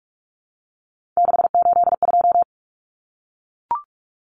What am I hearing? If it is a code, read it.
Morse code, heard as a keyed tone, about 44 wpm, 700 Hz. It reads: 682
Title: Common Cut Numbers - 3-Digit 45wpm